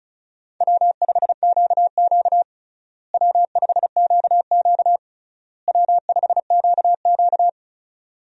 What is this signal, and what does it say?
Morse code, heard as a keyed tone, about 34 wpm, 700 Hz. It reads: W5QQ W5QQ W5QQ
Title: USA Call Signs - 4 Characters - Repeated 3x - 35wpm